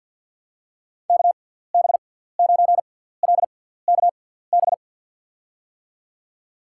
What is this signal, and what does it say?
Morse code, heard as a keyed tone, about 53 wpm, spaced as for 17 wpm, 700 Hz. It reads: KB9LKB
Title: USA Call Signs - 6 Characters - 6x Character Spacing - 50wpm